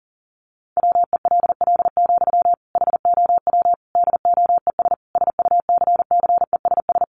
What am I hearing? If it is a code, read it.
Morse code, heard as a keyed tone, about 40 wpm, 700 Hz. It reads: WELL, HOW DOES SUCCESS
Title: Sentences from Top 1100 Words 40wpm